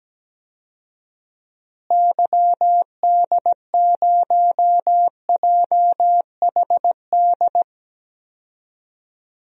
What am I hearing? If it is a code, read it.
Morse code, heard as a keyed tone, about 17 wpm, 700 Hz. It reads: YD0JHD
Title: DXCC-100 Call Signs - 6 Characters - Repeated 2x - 17wpm